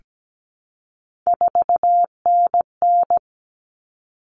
Morse code, heard as a keyed tone, about 17 wpm, 700 Hz. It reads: 4NN
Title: All Cut Numbers - 3-Digit 17wpm